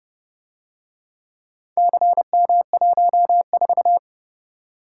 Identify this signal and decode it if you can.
Morse code, heard as a keyed tone, about 29 wpm, 700 Hz. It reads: CM14